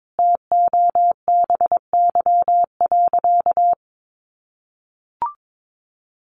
Morse code, heard as a keyed tone, about 22 wpm, 700 Hz. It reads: TOBY.
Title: Easy Phrases 22wpm